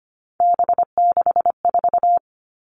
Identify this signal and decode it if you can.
Morse code, heard as a keyed tone, about 25 wpm, 700 Hz. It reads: B64
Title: All Cut Numbers - 3-Digit 25wpm